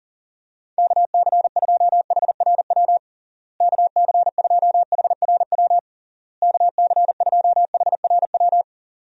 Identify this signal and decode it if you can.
Morse code, heard as a keyed tone, about 37 wpm, 700 Hz. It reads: KC2HRW KC2HRW KC2HRW
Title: USA Call Signs - 6 Characters - Repeated 3x - 40wpm